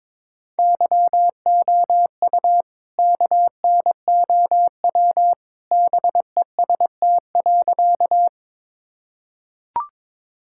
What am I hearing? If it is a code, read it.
Morse code, heard as a keyed tone, about 22 wpm, 700 Hz. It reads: YOU KNOW BEST.
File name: practice / 22wpm / Sentences with 2-Word Phrases - 22wpm